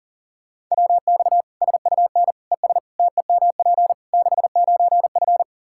Morse code, heard as a keyed tone, about 40 wpm, 700 Hz. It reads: WX SUN ES TEMP 69F
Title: K1USN SST QSO Practice 40wpm